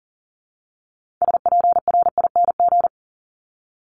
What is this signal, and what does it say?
Morse code, heard as a keyed tone, about 40 wpm, 700 Hz. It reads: SPRING
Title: Top 1001-1250 Words 40wpm